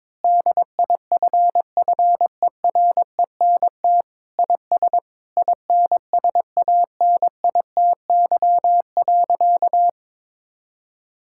Morse code, heard as a keyed tone, about 22 wpm, 700 Hz. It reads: DIFFERENT IS INSANITY.
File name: practice / 22wpm / Sentences with 8-Word Phrases - 22wpm